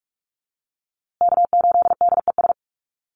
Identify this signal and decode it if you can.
Morse code, heard as a keyed tone, about 44 wpm, 700 Hz. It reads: K8DES
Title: USA Call Signs Repeated 3x 45wpm